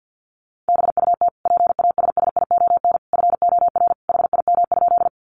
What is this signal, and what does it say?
Morse code, heard as a keyed tone, about 50 wpm, 700 Hz. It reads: BUT PASSION FOR HIM?
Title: Sentences from Top 1300 Words 50wpm